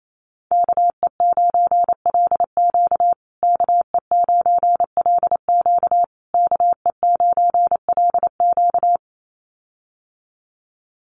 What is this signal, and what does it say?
Morse code, heard as a keyed tone, about 28 wpm, 700 Hz. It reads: KE9LQ KE9LQ KE9LQ
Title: USA Call Signs Repeated 3x 28wpm